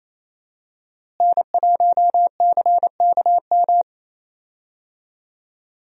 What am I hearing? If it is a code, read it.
Morse code, heard as a keyed tone, about 27 wpm, 700 Hz. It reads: N1CKM
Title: USA Call Signs - 5 Characters - Repeated 3x - 28wpm